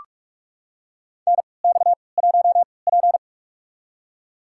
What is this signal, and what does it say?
Morse code, heard as a keyed tone, about 43 wpm, spaced as for 25 wpm, 700 Hz. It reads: NX1P
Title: USA Call Signs - 4 Characters - 3x Character Spacing - 45wpm